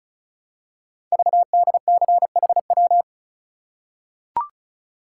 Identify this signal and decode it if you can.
Morse code, heard as a keyed tone, about 36 wpm, 700 Hz. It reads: VDCHW
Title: Code Groups - 5 Letters - 35wpm